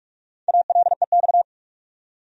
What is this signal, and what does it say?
Morse code, heard as a keyed tone, about 46 wpm, 700 Hz. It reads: ALEX
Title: USA Common Names - Max 4 Letters - 45wpm